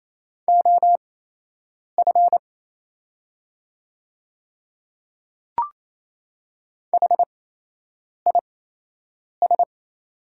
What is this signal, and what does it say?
Morse code, heard as a keyed tone, about 28 wpm, spaced as for 8 wpm, 700 Hz. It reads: OF HIS
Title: 8x Character Spacing - Top 500 Words - 28wpm